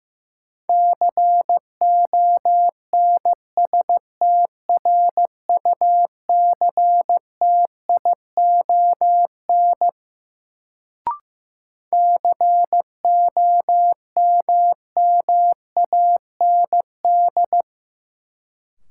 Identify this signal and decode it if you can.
Morse code, heard as a keyed tone, about 15 wpm, 700 Hz. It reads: CONSTRUCTION COMMAND
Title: Sentences from Top 1600 Words - New Word Drill - 15wpm